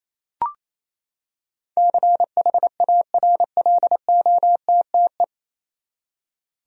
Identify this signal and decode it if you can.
Morse code, heard as a keyed tone, about 28 wpm, 700 Hz. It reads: CHARLOTTE
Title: USA Common Names - 28wpm